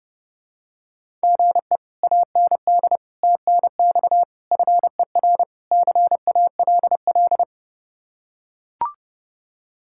Morse code, heard as a keyed tone, about 31 wpm, 700 Hz. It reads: GE AND TNX FER CALL